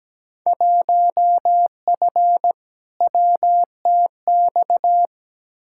Morse code, heard as a keyed tone, about 17 wpm, 700 Hz. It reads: 1F WTX
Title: Field Day Exchanges Encore - 17wpm